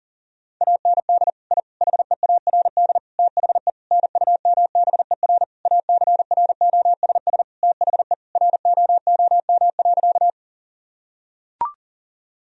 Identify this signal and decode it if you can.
Morse code, heard as a keyed tone, about 40 wpm, 700 Hz. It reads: AND I HEARD THE NUMBER ACROSS THE ROOM.
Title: Sentences from Top 1000 Words - Review Part 1 - 40wpm